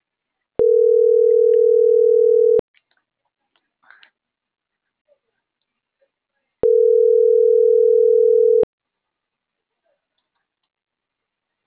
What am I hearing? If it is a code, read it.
Ringback tone (North America), 440 + 480 Hz.